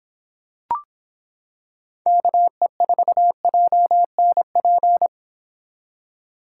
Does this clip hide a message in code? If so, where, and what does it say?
Yes: Morse code from 2.1 to 5.1 s, heard as a keyed tone, about 25 wpm, 700 Hz. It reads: KE4JNP